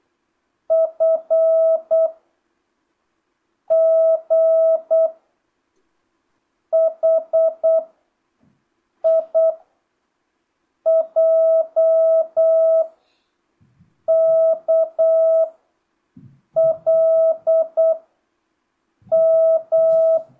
Morse code, heard as a keyed tone, about 8 wpm, 650 Hz. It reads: F G H I J K L M
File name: jh8cbh / 2025cw